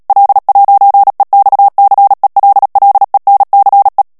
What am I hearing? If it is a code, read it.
Morse code, heard as a keyed tone, about 37 wpm, 800 Hz. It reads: L'EXCELLENCE
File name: CW / FICHIERS-MP3 / FRANCAIS TEXTE CLAIR 2 VITESSE 35 MOTS MINUTES